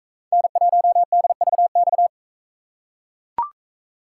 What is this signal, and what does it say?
Morse code, heard as a keyed tone, about 42 wpm, 700 Hz. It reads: N1DVX